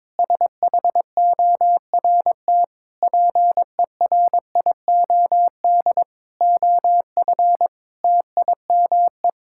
Morse code, heard as a keyed tone, about 22 wpm, 700 Hz. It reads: SHORT PERIOD OF TIME
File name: practice / 22wpm / Sets of 7 Words - 22wpm